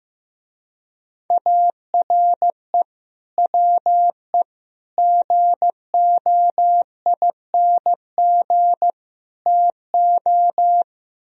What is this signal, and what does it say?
Morse code, heard as a keyed tone, about 16 wpm, 700 Hz. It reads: ARE WE GOING TO
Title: Speed-Racing - Sets of 4 Words 15wpm